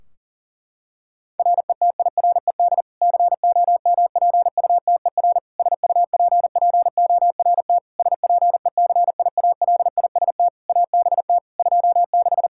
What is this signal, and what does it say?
Morse code, heard as a keyed tone, about 40 wpm, 700 Hz. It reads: RETIRED COMPUTER SUPPORT SPECIALIST ABT 26
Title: Practice QSO - Exchange 3 - More Info - 40wpm